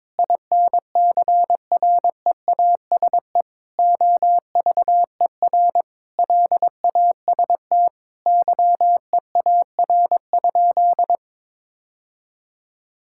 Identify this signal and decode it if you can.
Morse code, heard as a keyed tone, about 22 wpm, 700 Hz. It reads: INCREASE OVER LAST YEAR?
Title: Sentences from Top 1000 Words - Review Part 5 - 22wpm